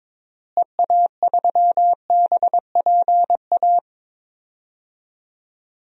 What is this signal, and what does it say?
Morse code, heard as a keyed tone, about 22 wpm, 700 Hz. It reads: EA3BPA
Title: DXCC-100 Call Signs - 6 Characters - Repeated 2x - 22wpm